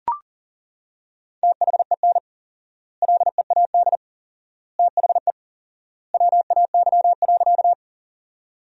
Morse code, heard as a keyed tone, about 40 wpm, 700 Hz. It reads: THEN LEAD THE WAY.
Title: 4x Word-Spacing - Sentences from Top 500 Words - 40wpm